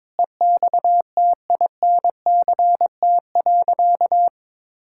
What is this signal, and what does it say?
Morse code, heard as a keyed tone, about 22 wpm, 700 Hz. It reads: EXTINCT.
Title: Pangram Challenge v2 - 22wpm